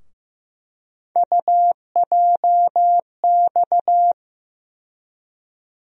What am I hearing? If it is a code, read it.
Morse code, heard as a keyed tone, about 15 wpm, 700 Hz. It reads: UJX
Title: USA Call Sign Postfixes - 3 Letters 15wpm